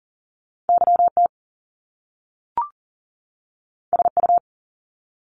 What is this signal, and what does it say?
Morse code, heard as a keyed tone, about 41 wpm, 700 Hz. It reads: YT SU
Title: Top Two Letter Combinations in Words 40wpm